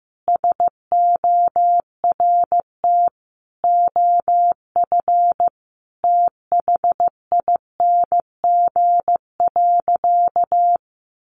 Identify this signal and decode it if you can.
Morse code, heard as a keyed tone, about 15 wpm, 700 Hz. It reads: SORT OF THING.
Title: Sentences from Top 1300 Words 15wpm